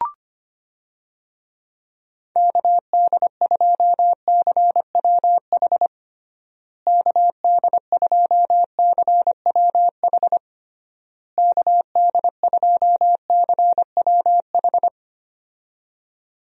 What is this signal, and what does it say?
Morse code, heard as a keyed tone, about 25 wpm, 700 Hz. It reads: KD2CWH KD2CWH KD2CWH
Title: USA Call Signs - 6 Characters - Repeated 3x - 25wpm